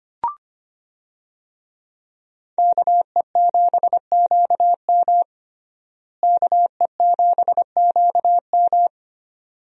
Morse code, heard as a keyed tone, about 25 wpm, 700 Hz. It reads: KE7QM KE7QM
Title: USA Call Signs - 5 Characters - Repeated 2x - 25wpm